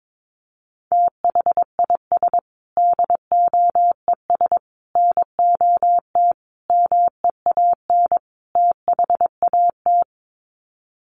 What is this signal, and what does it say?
Morse code, heard as a keyed tone, about 22 wpm, 700 Hz. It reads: THIS DOES NOT MEAN THAT
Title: Sets of 5 Words Encore2 22wpm